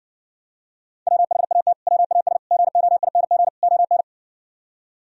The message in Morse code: WHAT WAS GOING ON

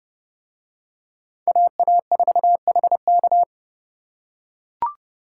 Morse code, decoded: AA4HK

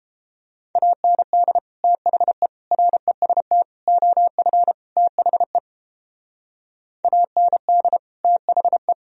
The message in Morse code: AND THE REST OF THE AND THE

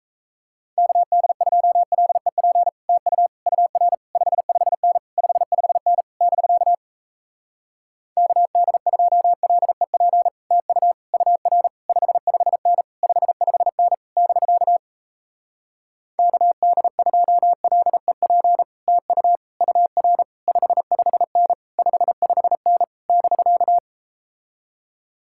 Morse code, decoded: KD2LEP TU UR 55N 55N <BK> KD2LEP TU UR 55N 55N <BK> KD2LEP TU UR 55N 55N <BK>

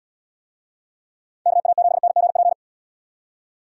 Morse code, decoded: KA7NGP

